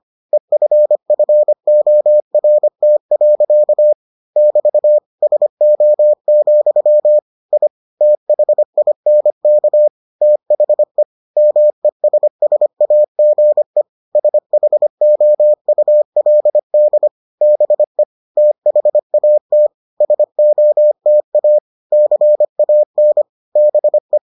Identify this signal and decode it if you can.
Morse code, heard as a keyed tone, about 25 wpm, 600 Hz. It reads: EFFORT. <BT> SO, I THINK THE MESSAGE SHOULD BE THAT SOTA CAN BE